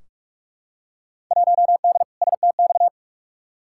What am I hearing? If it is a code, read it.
Morse code, heard as a keyed tone, about 46 wpm, 700 Hz. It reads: 1D STX
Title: Field Day Exchanges - 45wpm